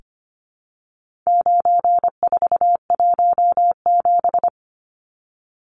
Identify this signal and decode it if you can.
Morse code, heard as a keyed tone, about 25 wpm, 700 Hz. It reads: 9417
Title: Numbers - 4-Digit 25wpm